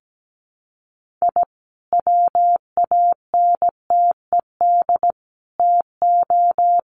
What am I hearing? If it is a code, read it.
Morse code, heard as a keyed tone, about 17 wpm, 700 Hz. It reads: I WANTED TO
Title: Sets of 5 Words Encore2 17wpm